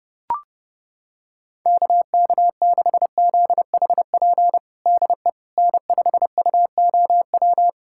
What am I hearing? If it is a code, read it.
Morse code, heard as a keyed tone, about 32 wpm, 700 Hz. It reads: KK6ZHP DE N5UOW
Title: USA Callsign DE Callsign 30wpm